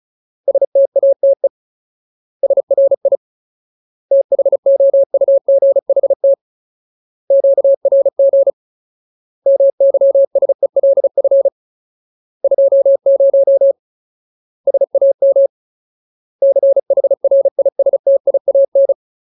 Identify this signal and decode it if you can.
Morse code, heard as a keyed tone, about 34 wpm, 550 Hz. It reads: STATE SRI THOUGHT QRG MYSELF 20 SAM CHRISTIAN
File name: Ultimate Word List / 35WPM Ultimate